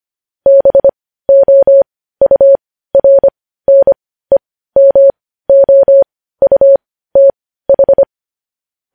Morse code, heard as a keyed tone, about 25 wpm, spaced as for 15 wpm, 550 Hz. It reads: BOURNEMOUTH
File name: podcast / News headlines-1x-25x15